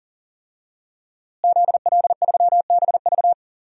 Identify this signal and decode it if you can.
Morse code, heard as a keyed tone, about 37 wpm, 700 Hz. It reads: ZL3BV
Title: DXCC-100 Call Signs - 5 Characters - Repeated 3x - 40wpm